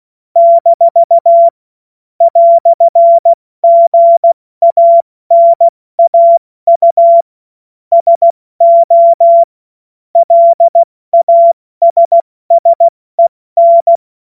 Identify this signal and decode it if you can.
Morse code, heard as a keyed tone, about 16 wpm, 680 Hz. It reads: - "GANAU SO LASSEN
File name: Morsekurs / Witze / Witz 09. Dezember 2025 70 F80